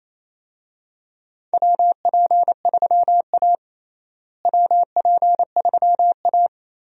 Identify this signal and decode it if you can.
Morse code, heard as a keyed tone, about 27 wpm, 700 Hz. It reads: WP3A WP3A
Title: USA Call Signs - 4 Characters - Repeated 2x - 28wpm